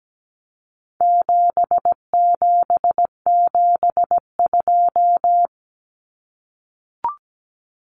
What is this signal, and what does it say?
Morse code, heard as a keyed tone, about 17 wpm, 700 Hz. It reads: 7772
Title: Numbers - 4-Digit 17wpm